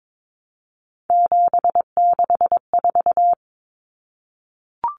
Morse code, heard as a keyed tone, about 22 wpm, 700 Hz. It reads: 764
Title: Numbers - 3-Digit 22wpm